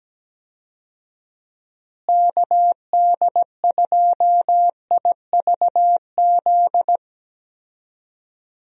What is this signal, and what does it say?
Morse code, heard as a keyed tone, about 17 wpm, 700 Hz. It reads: KD2IVZ
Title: USA Call Signs - 6 Characters - Repeated 2x - 17wpm